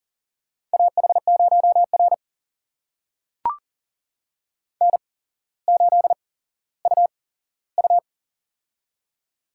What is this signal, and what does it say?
Morse code, heard as a keyed tone, about 42 wpm, spaced as for 17 wpm, 690 Hz. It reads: AH0R N8UU
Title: USA Call Signs - 4 Characters - 8x Character Spacing - 40wpm